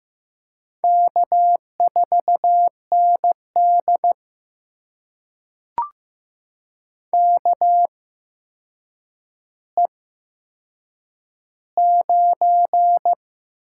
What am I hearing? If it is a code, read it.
Morse code, heard as a keyed tone, about 15 wpm, spaced as for 7 wpm, 700 Hz. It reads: K4ND KE9
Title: USA Call Signs - 4 Characters - 8x Character Spacing - 15wpm